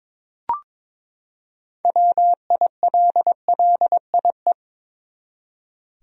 Morse code, heard as a keyed tone, about 22 wpm, 700 Hz. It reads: WILLIE
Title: USA Common Names - 22wpm